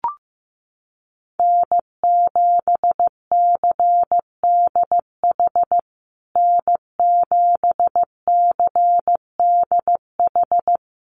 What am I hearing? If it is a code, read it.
Morse code, heard as a keyed tone, about 15 wpm, 700 Hz. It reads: N7CDH N7CDH
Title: USA Call Signs Repeated 3x 15wpm